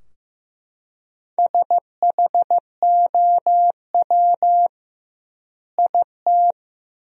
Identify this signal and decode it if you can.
Morse code, heard as a keyed tone, about 15 wpm, 700 Hz. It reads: SHOW IT
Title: 2x Word-Spacing - Sentences from Top 200 Words - 15wpm